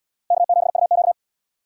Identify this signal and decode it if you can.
Morse code, heard as a keyed tone, about 71 wpm, 700 Hz. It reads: K7UQ